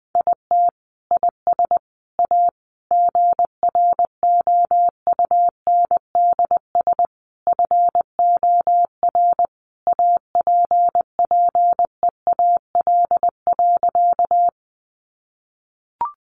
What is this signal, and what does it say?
Morse code, heard as a keyed tone, about 20 wpm, 700 Hz. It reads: IT IS A GROUNDS FOR APPEAL.
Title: Sentences from Top 1300 Words 20wpm